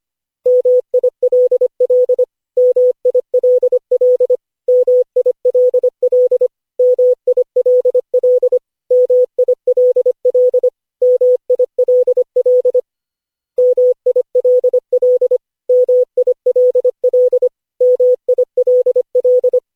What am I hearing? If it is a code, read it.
Morse code, heard as a keyed tone, about 25 wpm, 500 Hz. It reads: MILL MILL MILL MILL MILL MILL MILL MILL MILL
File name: cw / Session5 cw training